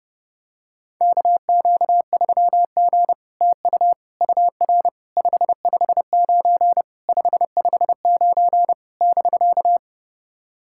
Morse code, heard as a keyed tone, about 30 wpm, 700 Hz. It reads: KQ3G TU UR 559 559 <BK>
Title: SOTA QSO Practice 30wpm